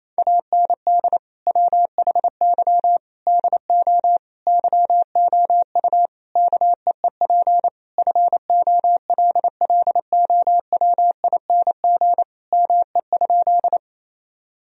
Morse code, heard as a keyed tone, about 27 wpm, 700 Hz. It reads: AND WHY DO YOU KEEP FOLLOWING ME?
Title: Sentences from Top 1000 Words - Review Part 4 - 28wpm